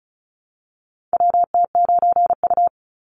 Morse code, heard as a keyed tone, about 36 wpm, 700 Hz. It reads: WT9U
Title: Call Signs - 35wpm